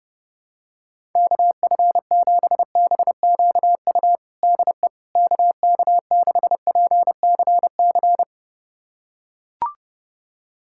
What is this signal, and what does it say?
Morse code, heard as a keyed tone, about 30 wpm, 700 Hz. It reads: KF7BQU DE KK6PCC
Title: USA Callsign DE Callsign 30wpm